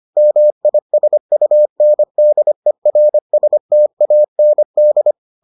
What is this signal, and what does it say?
Morse code, heard as a keyed tone, about 25 wpm, 600 Hz. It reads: MISUNDERSTAND